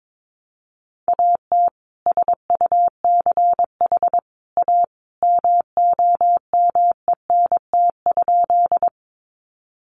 Morse code, heard as a keyed tone, about 22 wpm, 700 Hz. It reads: AT SUCH A MOMENT?
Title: Sentences from Top 1000 Words 22wpm